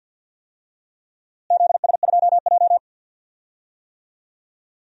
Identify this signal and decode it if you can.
Morse code, heard as a keyed tone, about 50 wpm, 700 Hz. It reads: ZS2J
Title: DXCC-100 Call Signs - 4 Characters - Repeated 3x - 50wpm